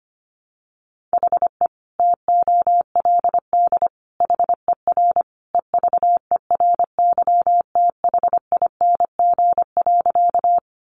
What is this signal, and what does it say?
Morse code, heard as a keyed tone, about 25 wpm, 700 Hz. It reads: HE TOLD HER EVERYTHING.
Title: Sentences from Top 700 Words 25wpm